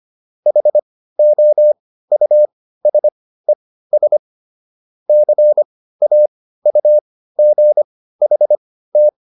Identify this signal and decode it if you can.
Morse code, heard as a keyed tone, about 25 wpm, spaced as for 15 wpm, 600 Hz. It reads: HOUSES CAUGHT